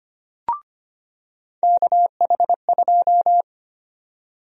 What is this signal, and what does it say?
Morse code, heard as a keyed tone, about 25 wpm, 700 Hz. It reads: KH2
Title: USA Call Sign Prefixes - 25wpm